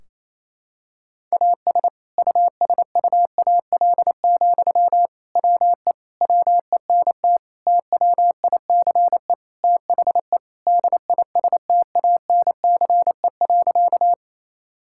Morse code, heard as a keyed tone, about 28 wpm, 700 Hz. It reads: AS USUAL, WE WENT TWICE THE DISTANCE.
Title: Sentences from Top 1600 Words 28wpm